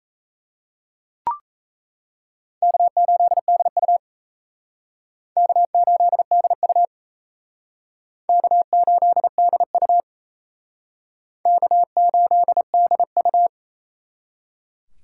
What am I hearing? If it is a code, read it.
Morse code, heard as a keyed tone, about 36 wpm, 700 Hz. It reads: K8DU K8DU K8DU K8DU